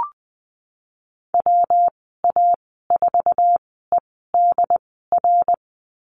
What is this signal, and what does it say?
Morse code, heard as a keyed tone, about 20 wpm, spaced as for 15 wpm, 700 Hz. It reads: WA4EDR
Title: 231 Lesson 37g - Letter x - Call Signs 20wpm